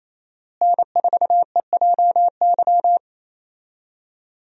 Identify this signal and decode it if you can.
Morse code, heard as a keyed tone, about 28 wpm, 700 Hz. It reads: N4EJY
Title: USA Call Signs - 5 Characters - Repeated 3x - 28wpm